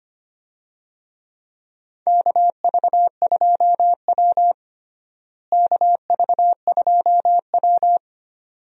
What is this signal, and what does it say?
Morse code, heard as a keyed tone, about 25 wpm, 700 Hz. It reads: KV2W KV2W
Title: USA Call Signs - 4 Characters - Repeated 2x - 25wpm